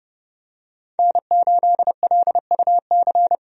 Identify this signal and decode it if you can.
Morse code, heard as a keyed tone, about 30 wpm, 700 Hz. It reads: N8LUC
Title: USA Call Signs - 5 Characters - Repeated 2x - 30wpm